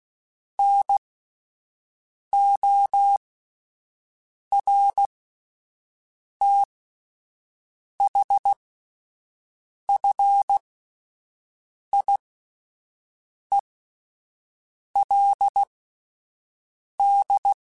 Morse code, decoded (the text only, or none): NORTHFIELD